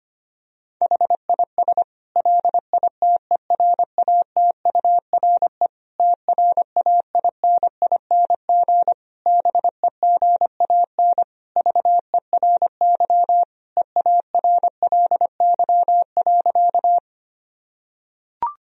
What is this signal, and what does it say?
Morse code, heard as a keyed tone, about 25 wpm, 700 Hz. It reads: HIS LITERATURE TRAINING BEGAN VERY EARLY.
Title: Sentences from Top 1000 Words - Review Part 4 - 25wpm